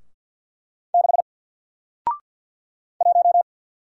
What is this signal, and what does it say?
Morse code, heard as a keyed tone, about 50 wpm, 700 Hz. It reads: 6 1
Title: Single Character (Letter:Number:Punctuation:Prosign) Alt 50wpm